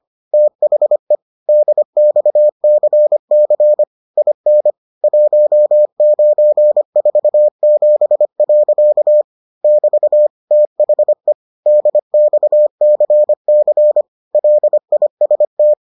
Morse code, heard as a keyed tone, about 25 wpm, 600 Hz. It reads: THE DXCC IN 1947. <BT> THE DXCC LIST